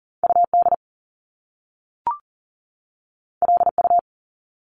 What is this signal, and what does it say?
Morse code, heard as a keyed tone, about 40 wpm, 700 Hz. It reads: UD LU
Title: Top Two Letter Combinations in Words 40wpm